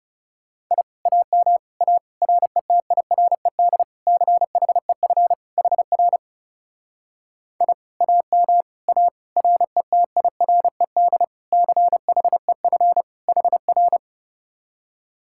Morse code, heard as a keyed tone, about 39 wpm, 700 Hz. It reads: I AM A RETIRED CHEF HR I AM A RETIRED CHEF HR